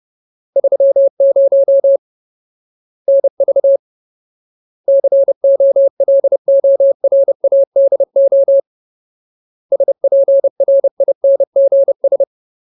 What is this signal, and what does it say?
Morse code, heard as a keyed tone, about 30 wpm, 550 Hz. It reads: 30 NV COLORADO SPRINGS
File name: Ultimate Word List / 30WPM Ultimate